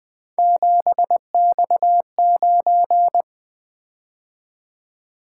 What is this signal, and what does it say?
Morse code, heard as a keyed tone, about 20 wpm, 700 Hz. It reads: 7X9
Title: Three Letter-Number Combinations - 20wpm